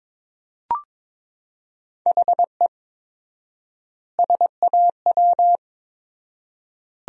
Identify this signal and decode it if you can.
Morse code, heard as a keyed tone, about 22 wpm, 700 Hz. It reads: HE SAW
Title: 4x Word-Spacing - Sentences from Top 500 Words - 22wpm